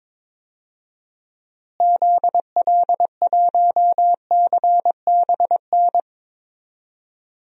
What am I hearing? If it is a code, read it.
Morse code, heard as a keyed tone, about 22 wpm, 700 Hz. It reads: ZL1CBN
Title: DXCC-100 Call Signs - 6 Characters - Repeated 3x - 22wpm